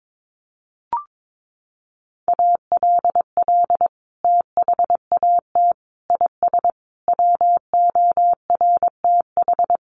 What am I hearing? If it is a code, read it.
Morse code, heard as a keyed tone, about 22 wpm, 700 Hz. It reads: ALL THAT IS WORTH
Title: Sentences from Top 1100 Words 22wpm